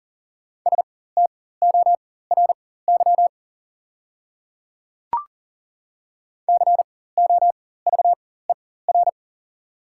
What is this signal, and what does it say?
Morse code, heard as a keyed tone, about 40 wpm, spaced as for 19 wpm, 700 Hz. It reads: STORY COVER
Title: 4x Character Spacing - Top 300 Words - 40wpm